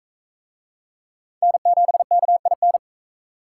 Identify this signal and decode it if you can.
Morse code, heard as a keyed tone, about 39 wpm, 700 Hz. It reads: N7KIN